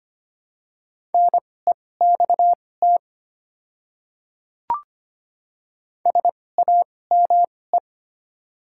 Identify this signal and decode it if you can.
Morse code, heard as a keyed tone, about 25 wpm, spaced as for 18 wpm, 700 Hz. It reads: NEXT SAME
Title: 2x Character Spacing - Top 500 Words - 25wpm